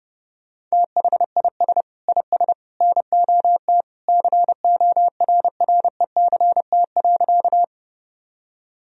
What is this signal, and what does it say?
Morse code, heard as a keyed tone, about 30 wpm, 700 Hz. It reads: THIS IS NOT CORRECT.
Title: Sentences from Top 1000 Words - Review Part 2 - 30wpm